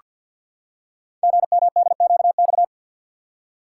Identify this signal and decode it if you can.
Morse code, heard as a keyed tone, about 48 wpm, 700 Hz. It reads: GMDQX
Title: Code Groups - 5 Letters - 50wpm